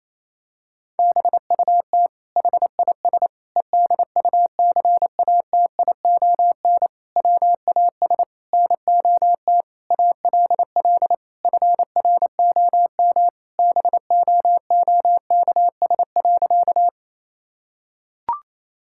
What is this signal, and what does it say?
Morse code, heard as a keyed tone, about 27 wpm, 700 Hz. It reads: BUT HIS EDUCATION WAS NOT ALL FROM BOOKS.
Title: Sentences from Top 1000 Words - Review Part 6 - 28wpm